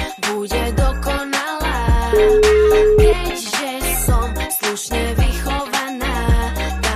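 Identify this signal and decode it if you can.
Ringback tone (Europe, 425 Hz), with noise.
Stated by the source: Ringback melody from O2 Knihovna that the caller hears before the call is answered.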